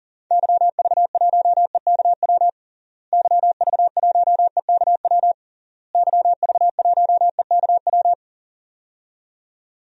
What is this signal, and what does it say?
Morse code, heard as a keyed tone, about 40 wpm, 700 Hz. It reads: YV1EKW YV1EKW YV1EKW
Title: DXCC-100 Call Signs - 6 Characters - Repeated 3x - 40wpm